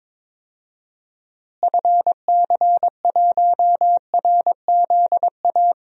Morse code, heard as a keyed tone, about 22 wpm, 700 Hz. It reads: FC1RZA
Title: DXCC-100 Call Signs - 6 Characters - Repeated 2x - 22wpm